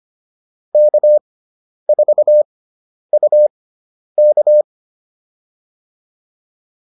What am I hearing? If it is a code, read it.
Morse code, heard as a keyed tone, about 25 wpm, spaced as for 10 wpm, 600 Hz. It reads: K4UK
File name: QSO208 10